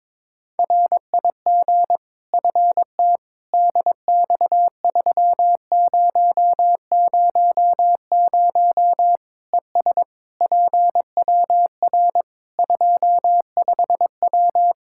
Morse code, heard as a keyed tone, about 22 wpm, 700 Hz. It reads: RIG FT DX3000 ES PWR 25W
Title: K1USN SST QSO Practice 22wpm